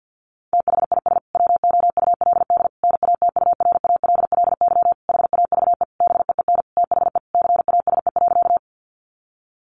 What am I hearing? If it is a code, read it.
Morse code, heard as a keyed tone, about 50 wpm, 700 Hz. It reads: THIS WOULD NATURALLY HAVE BEEN THE CASE.
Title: Sentences from Top 1300 Words 50wpm